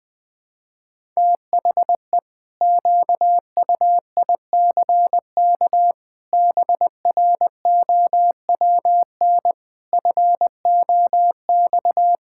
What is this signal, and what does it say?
Morse code, heard as a keyed tone, about 21 wpm, 700 Hz. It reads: THE QUICK BROWN FOX
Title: Pangram Challenge v2 - 20wpm